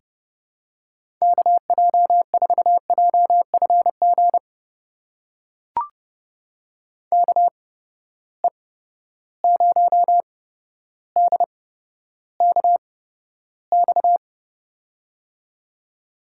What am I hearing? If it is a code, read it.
Morse code, heard as a keyed tone, about 32 wpm, spaced as for 13 wpm, 700 Hz. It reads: KJ4JFG KE0DKX
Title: USA Call Signs - 6 Characters - 8x Character Spacing - 30wpm